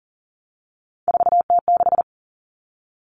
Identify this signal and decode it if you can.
Morse code, heard as a keyed tone, about 40 wpm, 700 Hz. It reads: 4T6
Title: All Cut Numbers - 3-Digit 40wpm